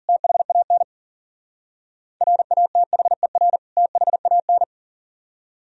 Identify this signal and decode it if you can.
Morse code, heard as a keyed tone, about 50 wpm, 700 Hz. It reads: THAN RATHER THAN